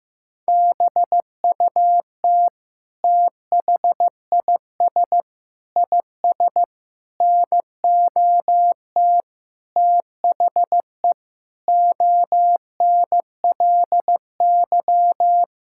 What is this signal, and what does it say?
Morse code, heard as a keyed tone, about 15 wpm, 700 Hz. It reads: BUT THIS IS NOT THE ONLY
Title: Sentences from Top 1600 Words 15wpm